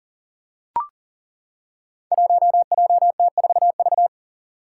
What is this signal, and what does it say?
Morse code, heard as a keyed tone, about 42 wpm, 700 Hz. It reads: 1JT4V
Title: Five Challenging Letter-Number Combinations 40wpm